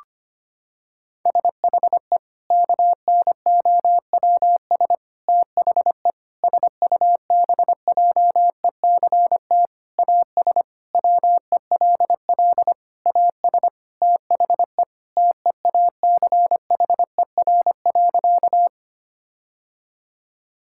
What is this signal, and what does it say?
Morse code, heard as a keyed tone, about 25 wpm, 700 Hz. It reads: SHE KNOWS THE SUBJECT AS WELL AS THE TEACHER.
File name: practice / 25wpm / Sentences with 4-Word Phrases - 25wpm